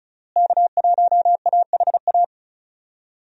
Morse code, heard as a keyed tone, about 36 wpm, 700 Hz. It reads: K1AHA
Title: USA Call Signs - 5 Characters - Repeated 3x - 35wpm